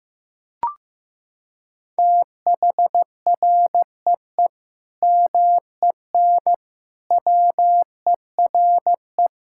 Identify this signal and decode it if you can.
Morse code, heard as a keyed tone, about 14 wpm, 700 Hz. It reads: THREE MEN WERE
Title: Sentences from Top 1600 Words 15wpm